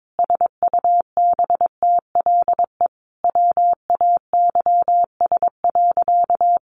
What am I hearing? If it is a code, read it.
Morse code, heard as a keyed tone, about 22 wpm, 700 Hz. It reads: SUBTLE WAYS.
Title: A Sound Of Thunder 22wpm